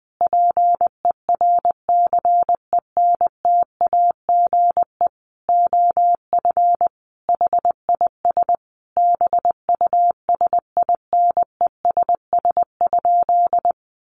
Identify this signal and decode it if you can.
Morse code, heard as a keyed tone, about 20 wpm, 700 Hz. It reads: PERCENTAGE OF HIS BUSINESS?
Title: Sentences from Top 1300 Words 20wpm